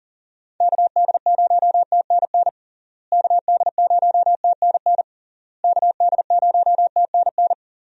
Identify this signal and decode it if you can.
Morse code, heard as a keyed tone, about 40 wpm, 700 Hz. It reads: KD0TNN KD0TNN KD0TNN
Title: USA Call Signs - 6 Characters - Repeated 3x - 40wpm